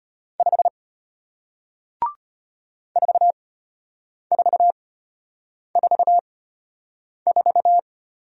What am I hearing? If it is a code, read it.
Morse code, heard as a keyed tone, about 34 wpm, spaced as for 8 wpm, 700 Hz. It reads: 5 4444